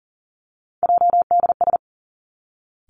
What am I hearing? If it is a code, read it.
Morse code, heard as a keyed tone, about 40 wpm, 700 Hz. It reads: JDS
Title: Three Challenging Letter-Number Combinations 40wpm